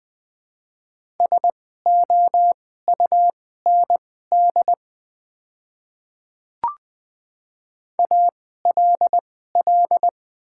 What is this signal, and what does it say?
Morse code, heard as a keyed tone, about 20 wpm, spaced as for 14 wpm, 700 Hz. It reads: SOUND ALL
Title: 2x Character Spacing - Top 100 Words - 20wpm